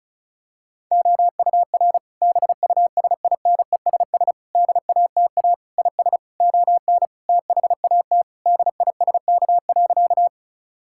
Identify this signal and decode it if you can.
Morse code, heard as a keyed tone, about 36 wpm, 690 Hz. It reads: OUR BUSINESS DATA IS ON THAT DISK.
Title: Sentences from Top 1000 Words - Review Part 6 - 35wpm